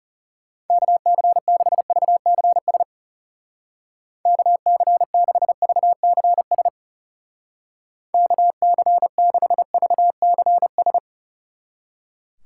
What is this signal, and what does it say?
Morse code, heard as a keyed tone, about 37 wpm, 700 Hz. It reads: KC6VCS KC6VCS KC6VCS
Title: Speed Racing - USA Call Signs 30wpm